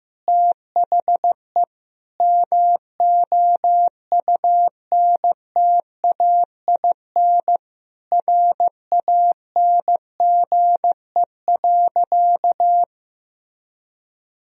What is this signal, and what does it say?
Morse code, heard as a keyed tone, about 15 wpm, 700 Hz. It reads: THE MOUNTAIN RANGE.
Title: Sentences from Top 1000 Words - Review Part 4 - 15wpm